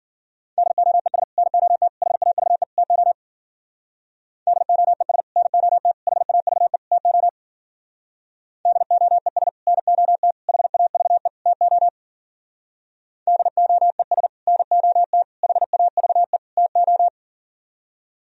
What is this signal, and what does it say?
Morse code, heard as a keyed tone, about 53 wpm, 700 Hz. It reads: DOES NOT HAVE TO DOES NOT HAVE TO DOES NOT HAVE TO DOES NOT HAVE TO